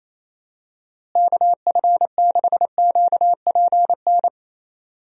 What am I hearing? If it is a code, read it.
Morse code, heard as a keyed tone, about 27 wpm, 700 Hz. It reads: KF6QPN
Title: USA Call Signs - 6 Characters - Repeated 2x - 28wpm